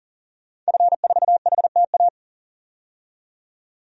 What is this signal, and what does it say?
Morse code, heard as a keyed tone, about 40 wpm, 700 Hz. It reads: F4HTA
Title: DXCC-100 Call Signs - 5 Characters - 40wpm